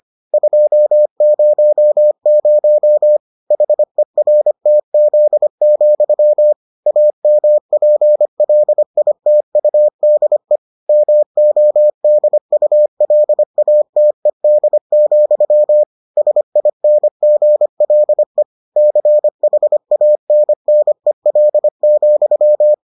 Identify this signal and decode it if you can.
Morse code, heard as a keyed tone, about 25 wpm, 600 Hz. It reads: 200 HERTZ, AMPLITUDE MODULATED, SINGLE CHANNEL,